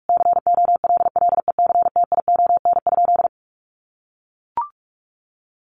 Morse code, heard as a keyed tone, about 44 wpm, 700 Hz. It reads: COLLECTION?
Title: Sentences from Top 1200 Words 45wpm